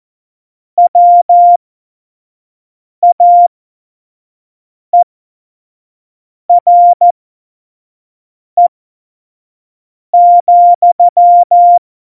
Morse code, heard as a keyed tone, about 14 wpm, spaced as for 5 wpm, 700 Hz. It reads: WAERE,
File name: Morsekurs / Witze / Witz 20. November 2018, 25 F70